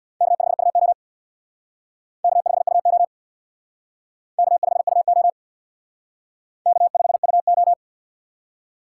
Morse code, heard as a keyed tone, about 62 wpm, 700 Hz. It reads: K5UO K5UO K5UO K5UO
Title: Speed Racing - USA 1x2 Call Signs 50wpm